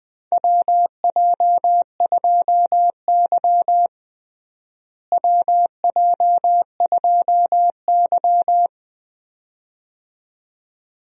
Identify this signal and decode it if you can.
Morse code, heard as a keyed tone, about 21 wpm, 700 Hz. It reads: WJ2Y WJ2Y
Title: USA Call Signs - 4 Characters - Repeated 2x - 20wpm